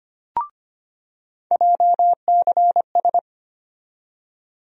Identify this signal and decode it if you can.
Morse code, heard as a keyed tone, about 25 wpm, 700 Hz. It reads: JCS
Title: USA Call Sign Postfixes - 3 Letters 25wpm